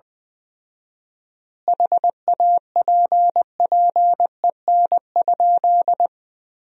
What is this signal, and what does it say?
Morse code, heard as a keyed tone, about 20 wpm, 700 Hz. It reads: HAPPEN?
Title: 4x Word-Spacing - Sentences from Top 400 Words - 20wpm